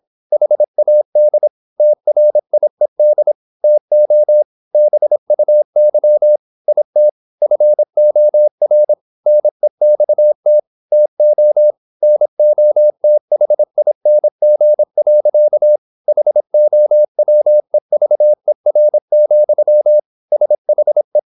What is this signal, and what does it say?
Morse code, heard as a keyed tone, about 26 wpm, 600 Hz. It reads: HAD TRIED TO BUY IT FOR NEXT TO NOTHING. HOWEVER, SHE